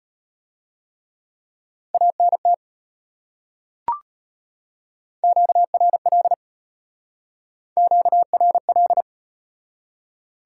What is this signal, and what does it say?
Morse code, heard as a keyed tone, about 36 wpm, 700 Hz. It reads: ANT QRL QRL